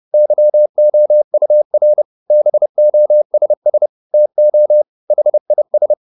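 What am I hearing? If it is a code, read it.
Morse code, heard as a keyed tone, about 30 wpm, 600 Hz. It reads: YOUR BOSS TO HIS
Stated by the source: Morse code at 30 wpm, the novel Never by Ken Follett in English, from CW med Gnister.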